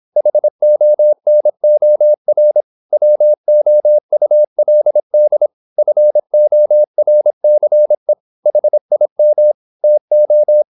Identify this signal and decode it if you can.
Morse code, heard as a keyed tone, about 26 wpm, 600 Hz. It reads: HONOR WOULD FORCE HIM TO
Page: Never 26wpm | CW med Gnister